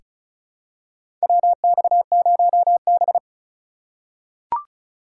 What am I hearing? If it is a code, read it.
Morse code, heard as a keyed tone, about 36 wpm, 700 Hz. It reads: WX0B
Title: USA Call Signs - 4 Characters - 3x Character Spacing - 35wpm